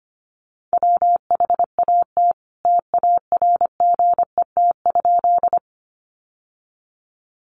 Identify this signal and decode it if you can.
Morse code, heard as a keyed tone, about 25 wpm, 700 Hz. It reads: WHAT TARGET?
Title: Sentences from Top 1200 Words 25wpm